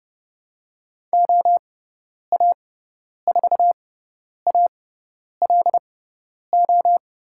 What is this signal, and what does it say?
Morse code, heard as a keyed tone, about 30 wpm, spaced as for 10 wpm, 700 Hz. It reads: OA4ALO
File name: Lesson 16g - Letter c - Call Signs 30wpm